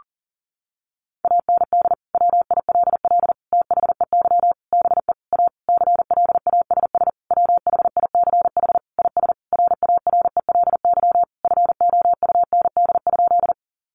Morse code, heard as a keyed tone, about 40 wpm, 700 Hz. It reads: AND WILL THEY BE A CLASS WHICH IS RARELY FOUND?
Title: Sentences from Top 1400 Words 40wpm